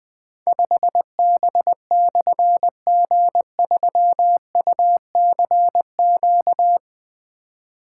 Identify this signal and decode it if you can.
Morse code, heard as a keyed tone, about 20 wpm, 700 Hz. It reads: 5B/G3UCQ
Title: Difficult Call Signs 20wpm